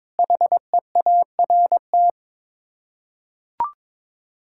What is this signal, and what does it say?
Morse code, heard as a keyed tone, about 21 wpm, 700 Hz. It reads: HEART
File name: practice / 22wpm / Sets of 6 Words - 22wpm